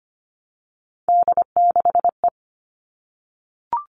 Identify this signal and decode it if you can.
Morse code, heard as a keyed tone, about 25 wpm, 700 Hz. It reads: D6E
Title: All Cut Numbers - 3-Digit 25wpm